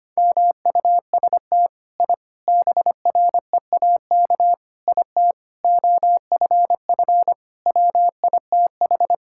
Morse code, heard as a keyed tone, about 25 wpm, 700 Hz. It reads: MUST I BREAK IT OFF WITH
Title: Sentences from Top 1000 Words - Review Part 6 - 25wpm